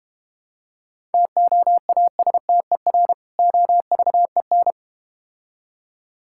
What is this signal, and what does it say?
Morse code, heard as a keyed tone, about 32 wpm, 700 Hz. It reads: TOASTER OVEN